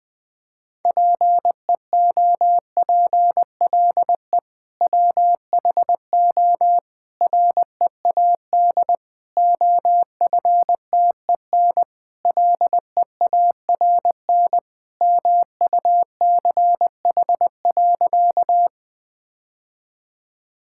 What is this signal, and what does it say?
Morse code, heard as a keyed tone, about 20 wpm, 700 Hz. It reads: PEOPLE WHO READ OFTEN LEARN MUCH.
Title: Sentences with 2-Word Phrases - 20wpm